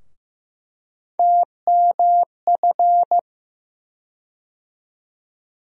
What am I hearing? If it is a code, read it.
Morse code, heard as a keyed tone, about 15 wpm, 700 Hz. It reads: TMF
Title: USA Call Sign Postfixes - 3 Letters 15wpm